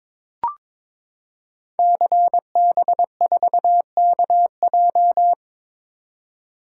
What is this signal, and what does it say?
Morse code, heard as a keyed tone, about 22 wpm, 700 Hz. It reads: CB4KJ
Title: Code Groups - 5 Letter:Numbers - 22wpm